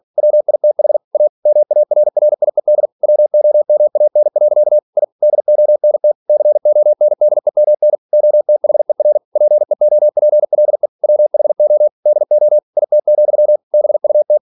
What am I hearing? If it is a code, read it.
Morse code, heard as a keyed tone, about 46 wpm, 600 Hz. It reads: WITH A MARRIED WOMAN. I DONT CONDEMN OTHER PEOPLE WHO DO IT, BUT
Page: Never 47wpm